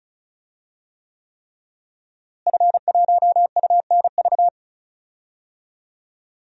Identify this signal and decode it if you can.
Morse code, heard as a keyed tone, about 34 wpm, 700 Hz. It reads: F1UNV
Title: DXCC-100 Call Signs - 5 Characters - Repeated 3x - 35wpm